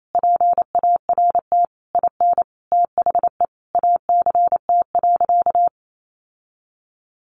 Morse code, heard as a keyed tone, about 28 wpm, 700 Hz. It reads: PART IN THE ACT.
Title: Sentences from Top 1100 Words 28wpm